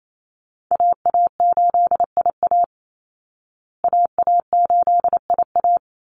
Morse code, heard as a keyed tone, about 28 wpm, 700 Hz. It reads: AA8IA AA8IA
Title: Call Signs Repeated 2x - 28wpm